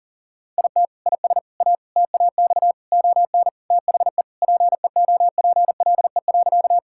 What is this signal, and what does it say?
Morse code, heard as a keyed tone, about 40 wpm, 700 Hz. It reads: IT IS A TAX ON THE PEOPLE.
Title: Sentences from Top 1000 Words - Review Part 5 - 40wpm